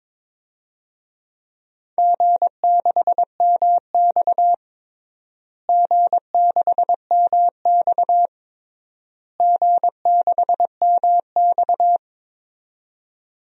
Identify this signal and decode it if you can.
Morse code, heard as a keyed tone, about 22 wpm, 700 Hz. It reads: G6MX G6MX G6MX
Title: DXCC-100 Call Signs - 4 Characters - Repeated 3x - 22wpm